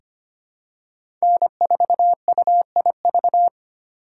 Morse code, heard as a keyed tone, about 25 wpm, 700 Hz. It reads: N4UIV
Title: USA Call Signs - 5 Characters - 2x Character Spacing - 25wpm